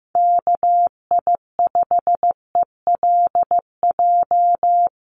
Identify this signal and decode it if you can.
Morse code, heard as a keyed tone, about 15 wpm, 700 Hz. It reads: KI5ELJ
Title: USA Call Signs Repeated 2x 15wpm